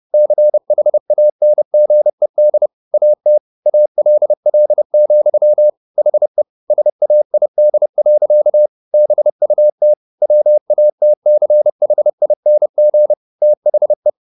Morse code, heard as a keyed tone, about 30 wpm, 600 Hz. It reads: CHANGED AT ALL, HE SAID. BUT WATCHING THE